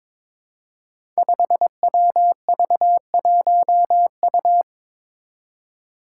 Morse code, heard as a keyed tone, about 22 wpm, 700 Hz. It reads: 5WV1U
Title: Five Challenging Letter-Number Combinations 22wpm